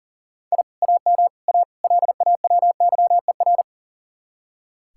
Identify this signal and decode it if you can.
Morse code, heard as a keyed tone, about 51 wpm, 700 Hz. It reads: I AM A LAWYER